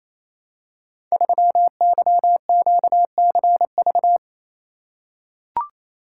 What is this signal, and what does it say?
Morse code, heard as a keyed tone, about 28 wpm, 700 Hz. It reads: 3YQCV
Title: Code Groups - 5 Letter:Numbers - 28wpm